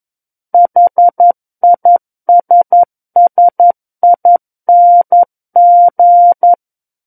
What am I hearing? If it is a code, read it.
Morse code, heard as a keyed tone, about 11 wpm, 700 Hz. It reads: HISSING
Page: SKCC - War of the Worlds - 02-Chapter 2 - 11 WPM